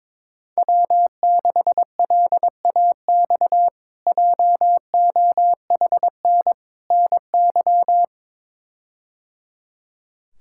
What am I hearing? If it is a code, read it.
Morse code, heard as a keyed tone, about 22 wpm, 700 Hz. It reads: W6LAX JOHN NY
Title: K1USN SST QSO Practice - 22wpm